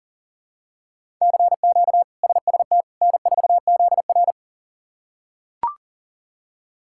Morse code, heard as a keyed tone, about 40 wpm, 700 Hz. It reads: CQ SST N4ZR
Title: K1USN SST QSO Practice - 40wpm